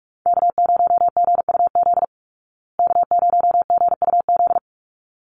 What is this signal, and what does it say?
Morse code, heard as a keyed tone, about 44 wpm, 700 Hz. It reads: K0GUZ K0GUZ
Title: Call Signs Repeated 3x - 45wpm